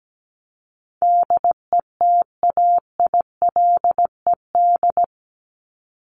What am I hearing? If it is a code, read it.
Morse code, heard as a keyed tone, about 17 wpm, 700 Hz. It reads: DETAILED
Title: Top 1251-1500 Words 17wpm